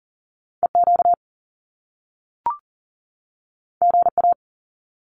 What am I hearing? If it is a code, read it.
Morse code, heard as a keyed tone, about 40 wpm, 700 Hz. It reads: EQ GA
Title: Top Two Letter Combinations in Words 40wpm